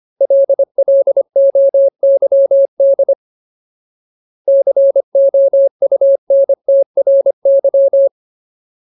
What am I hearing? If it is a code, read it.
Morse code, heard as a keyed tone, about 25 wpm, 550 Hz. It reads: LLOYD COUNTRY
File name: Ultimate Word List / 25WPM Ultimate